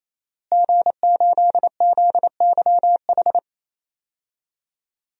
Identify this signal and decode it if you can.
Morse code, heard as a keyed tone, about 27 wpm, 700 Hz. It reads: G8ZYH
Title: DXCC-100 Call Signs - 5 Characters - Repeated 3x - 28wpm